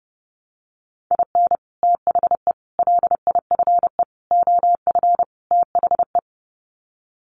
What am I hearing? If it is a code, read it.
Morse code, heard as a keyed tone, about 30 wpm, 700 Hz. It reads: IN THE LIFE OF THE
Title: Sets of 5 Words Encore2 30wpm